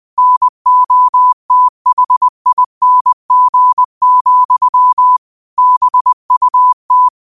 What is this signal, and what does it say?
Morse code, heard as a keyed tone, about 20 wpm, 1000 Hz. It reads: NOTHING, BUT
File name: qotdfiles / 20wpm